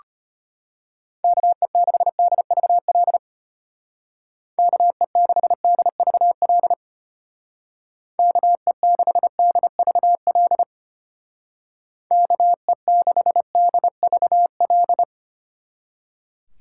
Morse code, heard as a keyed tone, about 32 wpm, 700 Hz. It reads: KE6DVL KE6DVL KE6DVL KE6DVL